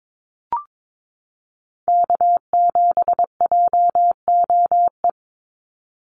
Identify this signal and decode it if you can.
Morse code, heard as a keyed tone, about 22 wpm, 700 Hz. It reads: K7JOE
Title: Call Signs Repeated 2x - 22wpm